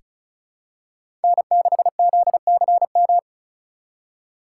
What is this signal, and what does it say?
Morse code, heard as a keyed tone, about 34 wpm, 700 Hz. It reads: N6ZCM
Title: USA Call Signs - 5 Characters - 2x Character Spacing - 35wpm